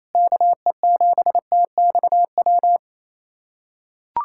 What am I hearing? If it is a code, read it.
Morse code, heard as a keyed tone, about 28 wpm, 700 Hz. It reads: KE7TXW
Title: USA Callsign DE Callsign 28wpm